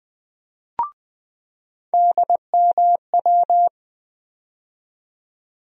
Morse code, heard as a keyed tone, about 20 wpm, 700 Hz. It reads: DMW
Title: USA Call Sign Postfixes - 3 Letters 20wpm